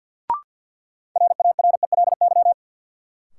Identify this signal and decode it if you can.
Morse code, heard as a keyed tone, about 52 wpm, 700 Hz. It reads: RARELY
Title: Sentences from Top 1400 Words - New Word Drill - 50wpm